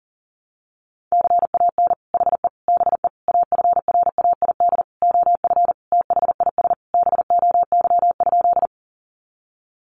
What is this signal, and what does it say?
Morse code, heard as a keyed tone, about 37 wpm, 700 Hz. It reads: CAN HE BE AFRAID OF THIS BOY?
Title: Sentences from Top 1500 Words 40wpm